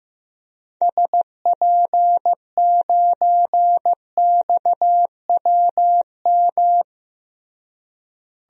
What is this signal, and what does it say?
Morse code, heard as a keyed tone, about 15 wpm, 700 Hz. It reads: SP9XWM
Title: DXCC-100 Call Signs - 6 Characters - Repeated 3x - 15wpm